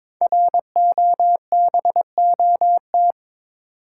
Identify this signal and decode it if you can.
Morse code, heard as a keyed tone, about 22 wpm, 700 Hz. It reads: ROBOT
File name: 5 Letter Words - First 3 Letters Given - 22wpm